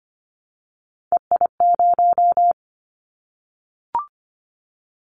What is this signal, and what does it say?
Morse code, heard as a keyed tone, about 25 wpm, 690 Hz. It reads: EI0
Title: Callsign Prefixes for Top 50 DXCC - 25wpm